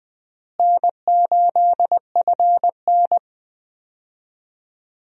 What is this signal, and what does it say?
Morse code, heard as a keyed tone, about 20 wpm, 700 Hz. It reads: N8FN
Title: USA Call Signs - 4 Characters - Repeated 2x - 20wpm